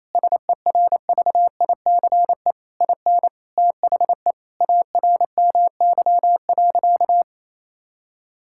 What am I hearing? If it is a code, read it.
Morse code, heard as a keyed tone, about 28 wpm, 700 Hz. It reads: SERVICE IN THE ARMY.
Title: Sentences from Top 1000 Words - Review Part 6 - 28wpm